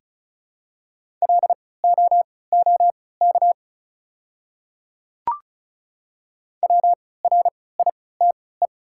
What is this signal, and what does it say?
Morse code, heard as a keyed tone, about 36 wpm, spaced as for 20 wpm, 700 Hz. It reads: LOOK WRITE
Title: 3x Character Spacing - Top 200 Words - 35wpm